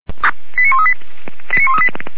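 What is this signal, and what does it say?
Five-tone selective call (ZVEI), heard as quick runs of five tones, tones of about 70 ms. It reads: 89138 89138
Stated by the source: ZVEI five-tone selective call received on an SDR.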